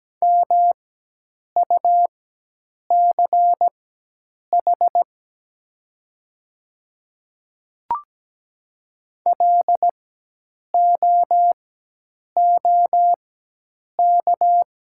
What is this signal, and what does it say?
Morse code, heard as a keyed tone, about 17 wpm, spaced as for 8 wpm, 700 Hz. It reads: MUCH LOOK
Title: 4x Character Spacing - Top 300 Words - 17wpm